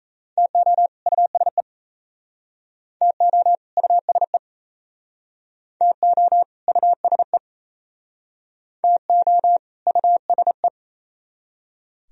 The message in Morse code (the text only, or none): TO USE TO USE TO USE TO USE